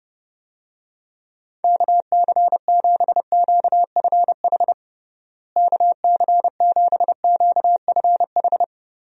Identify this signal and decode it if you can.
Morse code, heard as a keyed tone, about 30 wpm, 700 Hz. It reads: KC7QFH KC7QFH
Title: USA Call Signs - 6 Characters - Repeated 3x - 30wpm